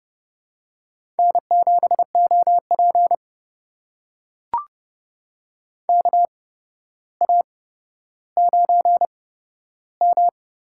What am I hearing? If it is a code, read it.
Morse code, heard as a keyed tone, about 30 wpm, spaced as for 13 wpm, 700 Hz. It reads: N7OP KA9M
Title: USA Call Signs - 4 Characters - 8x Character Spacing - 30wpm